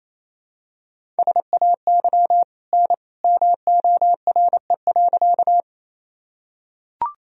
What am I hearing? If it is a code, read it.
Morse code, heard as a keyed tone, about 28 wpm, 700 Hz. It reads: SAY N MORE.
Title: Sentences from Top 200 Words and Abbreviations 28wpm